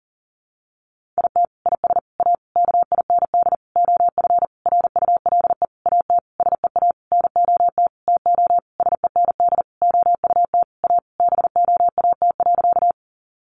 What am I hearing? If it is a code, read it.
Morse code, heard as a keyed tone, about 40 wpm, 700 Hz. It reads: IT IS A KIND OF RULE AT SEA NOT TO SEND OUT A BOAT.
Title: Sentences from Top 1500 Words 40wpm